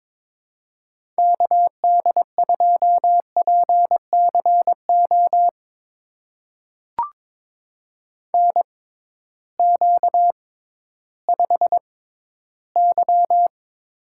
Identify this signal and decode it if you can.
Morse code, heard as a keyed tone, about 22 wpm, spaced as for 13 wpm, 700 Hz. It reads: KD2PCO NQ5Y
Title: USA Call Signs - 6x Character Spacing - 22wpm